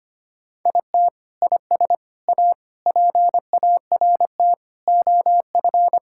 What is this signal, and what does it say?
Morse code, heard as a keyed tone, about 25 wpm, 700 Hz. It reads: IT IS A PART OF
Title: Sentences from Top 1000 Words - Review Part 5 - 25wpm